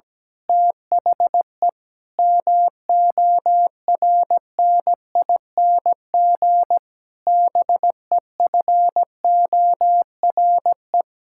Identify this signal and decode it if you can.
Morse code, heard as a keyed tone, about 17 wpm, 700 Hz. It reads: THE MORNING BEFORE
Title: Pangram Challenge v2 - 17wpm